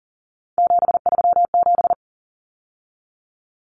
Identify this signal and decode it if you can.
Morse code, heard as a keyed tone, about 39 wpm, 700 Hz. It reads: 737
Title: Numbers - 3-Digit 40wpm